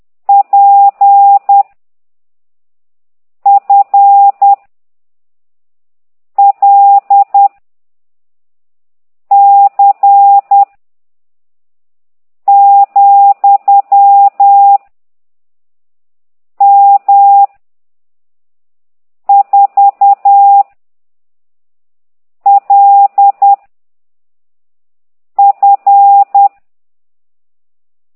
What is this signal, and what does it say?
Morse code, heard as a keyed tone, about 10 wpm, 800 Hz. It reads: P F L C , M 4 L F